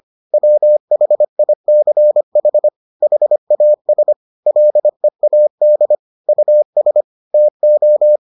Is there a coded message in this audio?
Yes: Morse code from 0.3 to 8.2 s, heard as a keyed tone, about 25 wpm, 600 Hz. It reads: WHICH HAS LEAD US TO